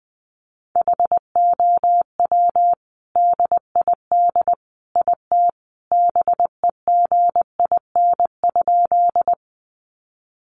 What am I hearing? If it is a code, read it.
Morse code, heard as a keyed tone, about 20 wpm, 700 Hz. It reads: HOW DID IT BEGIN?
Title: Sentences from Top 300 Words 20wpm